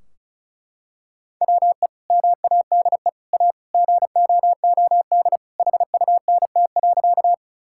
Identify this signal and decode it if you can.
Morse code, heard as a keyed tone, about 36 wpm, 700 Hz. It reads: WE MADE A GOOD HUNT.
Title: Sentences from Top 1600 Words 35wpm